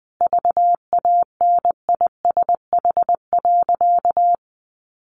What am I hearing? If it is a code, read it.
Morse code, heard as a keyed tone, about 20 wpm, 700 Hz. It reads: VANISH.
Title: A Sound Of Thunder 20wpm